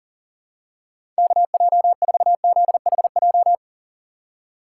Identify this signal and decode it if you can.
Morse code, heard as a keyed tone, about 40 wpm, 700 Hz. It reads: KJ4ZHJ
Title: USA Call Signs - 6 Characters - 3x Character Spacing - 40wpm